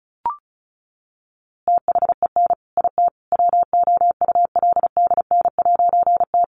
Morse code, heard as a keyed tone, about 34 wpm, 700 Hz. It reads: THEN IT WOULDN'T
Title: A Sound Of Thunder 35wpm